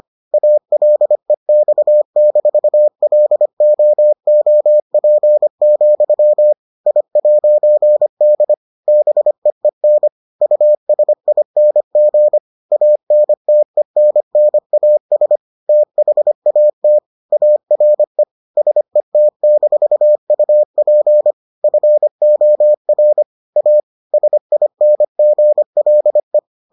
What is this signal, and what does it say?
Morse code, heard as a keyed tone, about 25 wpm, 600 Hz. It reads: ALEX-LOOP, I'D BEEN USING ANTENNAS THAT ARE SET-UP FOR A SINGLE